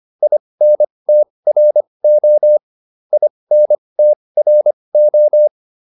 Morse code, heard as a keyed tone, about 25 wpm, spaced as for 20 wpm, 600 Hz. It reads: INTRO INTRO